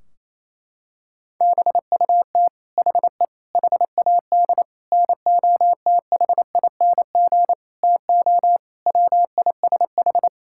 Morse code, heard as a keyed tone, about 28 wpm, 700 Hz. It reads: BUT HE HAD NOTHING TO WISH
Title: Sentences from Top 1600 Words 28wpm